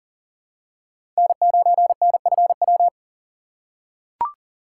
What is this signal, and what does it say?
Morse code, heard as a keyed tone, about 38 wpm, 700 Hz. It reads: N9NFW
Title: USA Call Signs - 3x Character Spacing - 40wpm